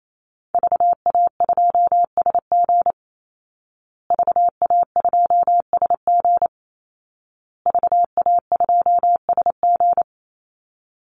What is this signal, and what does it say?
Morse code, heard as a keyed tone, about 28 wpm, 700 Hz. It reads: VA2SG VA2SG VA2SG
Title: Call Signs Repeated 2x - 28wpm